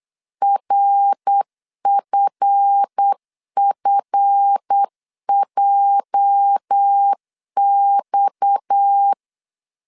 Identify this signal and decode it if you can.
Morse code, heard as a keyed tone, about 8 wpm, 800 Hz. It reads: RFFJX
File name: CW / DET / 1 9 1500C S B 8wpm 5624